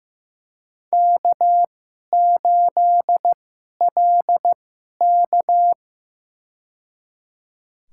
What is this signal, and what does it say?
Morse code, heard as a keyed tone, about 15 wpm, spaced as for 11 wpm, 700 Hz. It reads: K8LK
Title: USA Call Signs - 2x Character Spacing - 15wpm